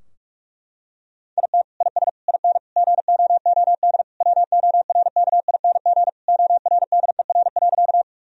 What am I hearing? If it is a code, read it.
Morse code, heard as a keyed tone, about 41 wpm, 700 Hz. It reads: IT IS IN GOOD WORKING ORDER.
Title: Sentences from Top 1000 Words - Review Part 4 - 45wpm